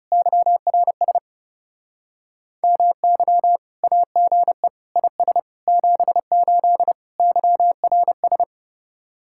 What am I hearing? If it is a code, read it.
Morse code, heard as a keyed tone, about 40 wpm, 700 Hz. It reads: YRS MY AGE IS 78 YRS